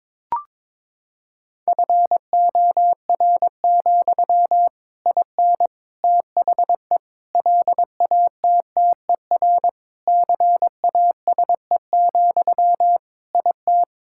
Morse code, heard as a keyed tone, about 22 wpm, 700 Hz. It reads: FOR, IN THE LATTER CASE, IT
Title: Sentences from Top 1600 Words 22wpm